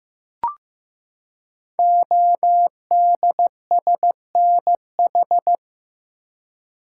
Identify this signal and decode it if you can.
Morse code, heard as a keyed tone, about 15 wpm, 700 Hz. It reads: ODSNH
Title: Code Groups - 5 Letter:Numbers - 15wpm